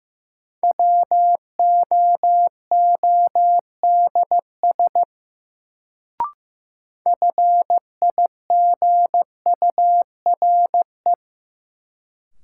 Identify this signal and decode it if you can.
Morse code, heard as a keyed tone, about 15 wpm, 700 Hz. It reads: WOODS FIGURE
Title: Sentences from Top 1000 Words - Review Part 2 - New Word Drill - 15wpm